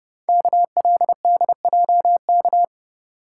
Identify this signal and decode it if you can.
Morse code, heard as a keyed tone, about 30 wpm, 690 Hz. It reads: KLDJK
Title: Code Groups - 5 Letters - 30wpm